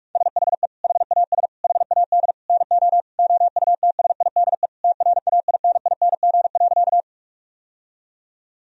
Morse code, heard as a keyed tone, about 41 wpm, 700 Hz. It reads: SHE HAS HAD NO OUTSIDE TRAINING.
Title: Sentences from Top 1000 Words - Review Part 4 - 45wpm